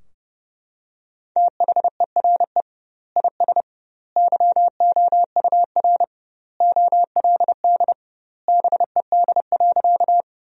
Morse code, heard as a keyed tone, about 29 wpm, 700 Hz. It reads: THERE IS YOUR OLD BED.
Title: 2x Word-Spacing - Sentences from Top 500 Words - 30wpm